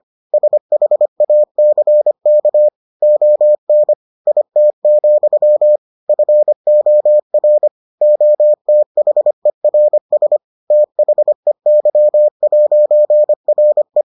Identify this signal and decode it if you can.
Morse code, heard as a keyed tone, about 25 wpm, 600 Hz. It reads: SHACK ON IT, FOR OTHERS THEY'RE